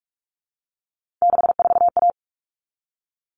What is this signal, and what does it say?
Morse code, heard as a keyed tone, about 44 wpm, 700 Hz. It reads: 64A
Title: All Cut Numbers - 3-Digit 45wpm